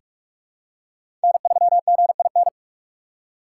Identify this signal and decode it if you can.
Morse code, heard as a keyed tone, about 45 wpm, 700 Hz. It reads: N3GIN
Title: USA Call Signs - 5 Characters - 2x Character Spacing - 45wpm